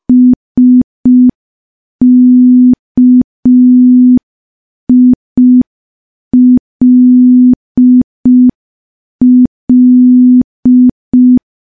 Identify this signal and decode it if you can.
Morse code, heard as a keyed tone, about 5 wpm, 250 Hz. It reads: SKILL
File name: morse / morse-training-05WPM